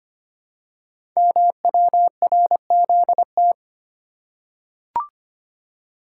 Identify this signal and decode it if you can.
Morse code, heard as a keyed tone, about 25 wpm, 700 Hz. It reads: MWRZT
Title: Code Groups - 5 Letters - 25wpm